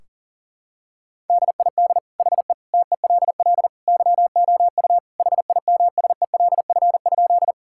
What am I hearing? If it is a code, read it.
Morse code, heard as a keyed tone, about 40 wpm, 700 Hz. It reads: DID HE TELL YOU HIMSELF?
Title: Sentences from Top 1600 Words 40wpm